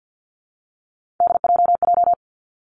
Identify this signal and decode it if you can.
Morse code, heard as a keyed tone, about 50 wpm, 700 Hz. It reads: DJJ
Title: Three Challenging Letter-Number Combinations 50wpm